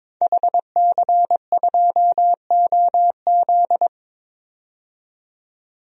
Morse code, heard as a keyed tone, about 23 wpm, 700 Hz. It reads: HC2OZ
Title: DXCC-100 Call Signs - 5 Characters - 22wpm